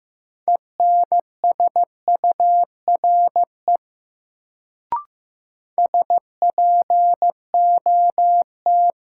Morse code, heard as a keyed tone, about 15 wpm, 700 Hz. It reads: ENSURE SPOT
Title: Sentences from Top 1100 Words - New Word Drill - 15wpm